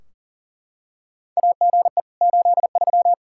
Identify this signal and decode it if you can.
Morse code, heard as a keyed tone, about 37 wpm, 700 Hz. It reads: AGE 83
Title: Practice QSO - Exchange 3 - More Info - 40wpm